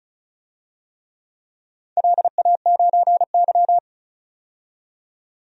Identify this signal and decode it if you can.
Morse code, heard as a keyed tone, about 36 wpm, 700 Hz. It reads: LA9Y
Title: DXCC-100 Call Signs - 4 Characters - 35wpm